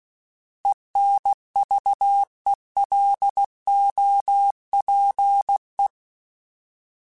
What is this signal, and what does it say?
Morse code, heard as a keyed tone, about 16 wpm, 800 Hz. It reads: ENVELOPE
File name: Alphabetical / Alphabetical mp3s / Es @15